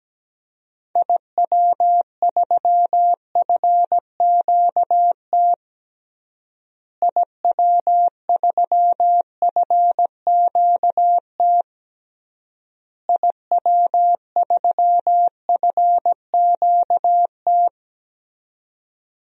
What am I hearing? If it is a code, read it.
Morse code, heard as a keyed tone, about 17 wpm, 700 Hz. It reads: IW3FQT IW3FQT IW3FQT
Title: DXCC-100 Call Signs - 6 Characters - Repeated 3x - 17wpm